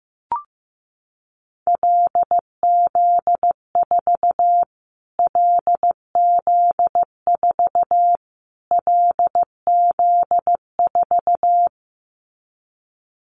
Morse code, heard as a keyed tone, about 15 wpm, 700 Hz. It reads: LZ4 LZ4 LZ4
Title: Callsign Prefixes for Top 25 DXCC - Repeated 3x - 15wpm